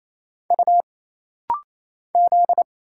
Morse code, heard as a keyed tone, about 27 wpm, 700 Hz. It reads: U Z
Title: Single Letters - Rapid-Fire 28wpm